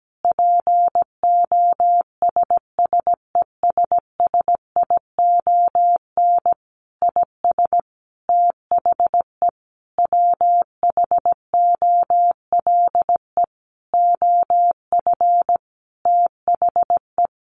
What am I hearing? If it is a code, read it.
Morse code, heard as a keyed tone, about 17 wpm, 700 Hz. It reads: POSSESSION IS THE WHOLE OF THE
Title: Sentences from Top 1400 Words 17wpm